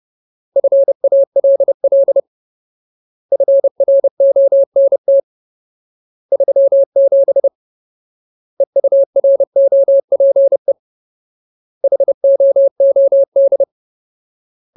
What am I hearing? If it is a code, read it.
Morse code, heard as a keyed tone, about 30 wpm, 550 Hz. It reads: FALL FRONT 37 EUROPE HOOD
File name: Ultimate Word List / 30WPM Ultimate